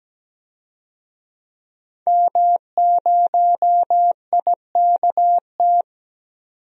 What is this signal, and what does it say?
Morse code, heard as a keyed tone, about 17 wpm, 700 Hz. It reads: M0IKT
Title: DXCC-100 Call Signs - 5 Characters - 17wpm